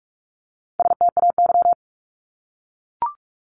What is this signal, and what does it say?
Morse code, heard as a keyed tone, about 44 wpm, 700 Hz. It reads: STAY
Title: Top 500 Words Repeated 45wpm